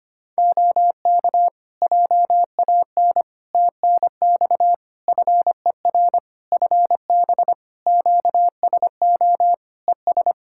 Morse code, heard as a keyed tone, about 25 wpm, 700 Hz. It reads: OK JAN TNX FER FB QSO ES
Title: K1USN SST QSO Practice 25wpm